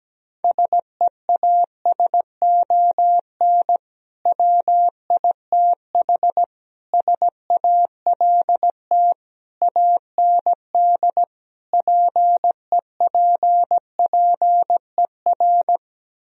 Morse code, heard as a keyed tone, about 17 wpm, 700 Hz. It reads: SEASON WITH SALT AND PEPPER
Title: Speed-Racing - Sets of 5 Words 17wpm